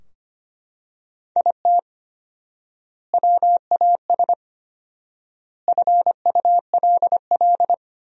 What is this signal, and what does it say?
Morse code, heard as a keyed tone, about 25 wpm, 700 Hz. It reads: IT WAS FULL
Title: 4x Word-Spacing - Sentences from Top 500 Words - 25wpm